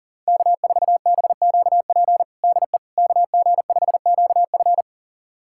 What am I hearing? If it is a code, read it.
Morse code, heard as a keyed tone, about 40 wpm, 700 Hz. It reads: K4BQP DE KG5QF
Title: USA Callsign DE Callsign 40wpm